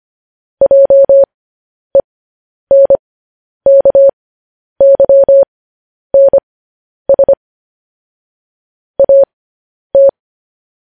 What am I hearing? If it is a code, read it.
Morse code, heard as a keyed tone, about 25 wpm, spaced as for 10 wpm, 550 Hz. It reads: JENKYNS AT
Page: news headlines in morse code at 10 wpm